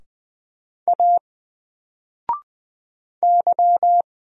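Morse code, heard as a keyed tone, about 20 wpm, 700 Hz. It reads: A Y
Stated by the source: Single letter and number Morse drill at 20 wpm, from N4LCD.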